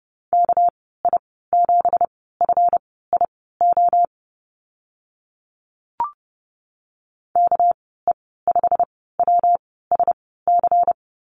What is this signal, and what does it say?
Morse code, heard as a keyed tone, about 30 wpm, spaced as for 17 wpm, 700 Hz. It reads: KI7FIO KE5WSC
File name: Lesson 37g - Letter x - Call Signs 30wpm